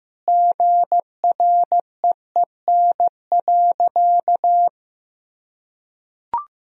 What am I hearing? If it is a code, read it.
Morse code, heard as a keyed tone, about 15 wpm, 700 Hz. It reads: GREEN.
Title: Sentences from Top 1000 Words - Review Part 2 - 15wpm